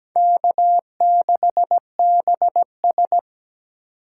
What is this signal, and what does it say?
Morse code, heard as a keyed tone, about 17 wpm, 700 Hz. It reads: K6BS
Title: USA Call Signs - 4 Characters - 3x Character Spacing - 17wpm